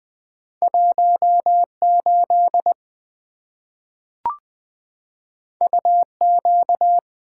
Morse code, heard as a keyed tone, about 20 wpm, 700 Hz. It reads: 18 UQ
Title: Two Letter-Number Combinations - 20wpm